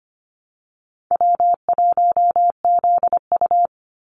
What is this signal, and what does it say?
Morse code, heard as a keyed tone, about 25 wpm, 690 Hz. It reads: W1ZU
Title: NA SOTA Activator Callsigns 25wpm